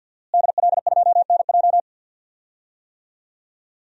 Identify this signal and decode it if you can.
Morse code, heard as a keyed tone, about 51 wpm, 700 Hz. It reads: DL2NJ
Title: DXCC-100 Call Signs - 5 Characters - Repeated 2x - 50wpm